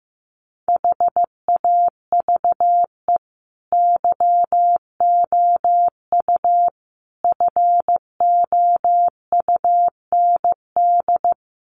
Morse code, heard as a keyed tone, about 15 wpm, 700 Hz. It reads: HAVE YOU FOUND
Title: Sentences from Top 300 Words 15wpm